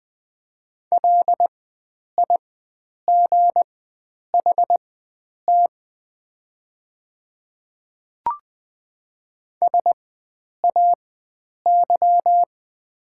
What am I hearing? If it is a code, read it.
Morse code, heard as a keyed tone, about 20 wpm, spaced as for 9 wpm, 700 Hz. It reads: LIGHT SAY
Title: 4x Character Spacing - Top 300 Words - 20wpm